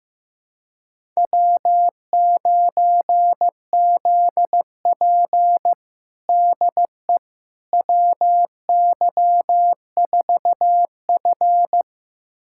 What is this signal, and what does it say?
Morse code, heard as a keyed tone, about 15 wpm, 700 Hz. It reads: W9ZP DE WY4F
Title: USA Callsign DE Callsign - 4 Characters - 15wpm